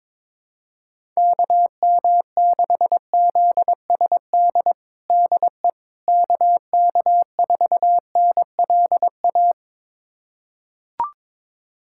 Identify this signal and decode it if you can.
Morse code, heard as a keyed tone, about 22 wpm, 700 Hz. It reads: KM6ZSD DE KK4NLA
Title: USA Callsign DE Callsign - 6 Characters - 22wpm